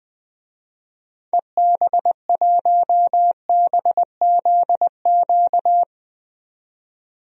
Morse code, heard as a keyed tone, about 20 wpm, 700 Hz. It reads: EB1BZQ
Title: DXCC-100 Call Signs - 6 Characters - 20wpm